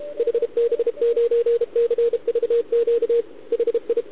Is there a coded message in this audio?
Yes: Morse code from 0.2 to 4.0 s, heard as a keyed tone, about 31 wpm, 470 Hz, with radio noise. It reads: HB9CVQ HI